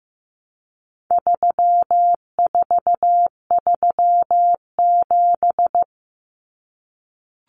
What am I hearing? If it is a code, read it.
Morse code, heard as a keyed tone, about 15 wpm, 700 Hz. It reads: 3437
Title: Common Cut Numbers - 4-Digit 15wpm